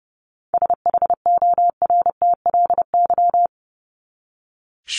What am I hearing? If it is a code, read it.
Morse code, heard as a keyed tone, about 30 wpm, 700 Hz. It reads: SHORTLY
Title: LY Suffix - Top 100 Words with Base Words 30wpm